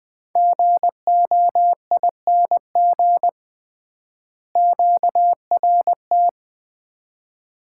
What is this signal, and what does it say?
Morse code, heard as a keyed tone, about 20 wpm, 700 Hz. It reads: GOING QRT
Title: Common QSO Phrases - 3x Word-Spacing - 20wpm